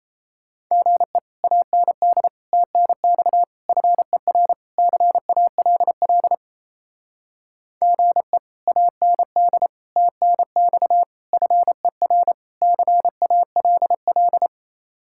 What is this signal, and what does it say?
Morse code, heard as a keyed tone, about 36 wpm, 700 Hz. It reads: GE AND TNX FER CALL GE AND TNX FER CALL